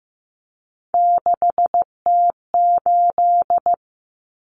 Morse code, heard as a keyed tone, about 15 wpm, 700 Hz. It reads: 6T8
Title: Common Cut Numbers - 3-Digit 15wpm